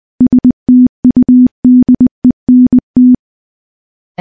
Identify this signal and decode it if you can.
Morse code, heard as a keyed tone, about 20 wpm, 250 Hz. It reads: STUDENT
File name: morse / morse-training-20WPM